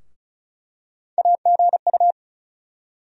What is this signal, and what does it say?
Morse code, heard as a keyed tone, about 36 wpm, 700 Hz. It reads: AGU
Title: USA Call Sign Postfixes - 3 Letters 35wpm